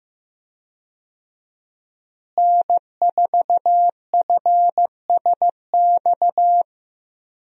Morse code, heard as a keyed tone, about 15 wpm, 700 Hz. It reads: N4FSX
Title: USA Call Signs - 5 Characters - Repeated 3x - 15wpm